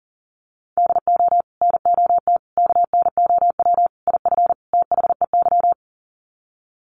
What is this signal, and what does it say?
Morse code, heard as a keyed tone, about 40 wpm, 700 Hz. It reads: DO NOT KNOW IF THEY
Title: Sets of 5 Words Encore 40wpm